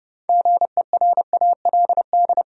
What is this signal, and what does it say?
Morse code, heard as a keyed tone, about 31 wpm, 700 Hz. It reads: GERALD
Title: USA Common Names - 30wpm